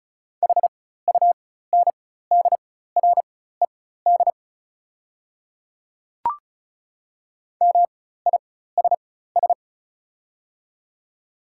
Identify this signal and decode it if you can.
Morse code, heard as a keyed tone, about 34 wpm, spaced as for 16 wpm, 700 Hz. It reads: HUNDRED MISS
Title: 4x Character Spacing - Top 500 Words - 35wpm